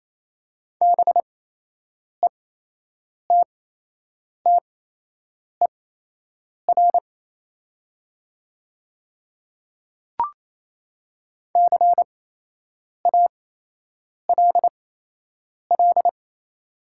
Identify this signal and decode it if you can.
Morse code, heard as a keyed tone, about 28 wpm, spaced as for 8 wpm, 700 Hz. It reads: BETTER CALL
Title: 8x Character Spacing - Top 500 Words - 28wpm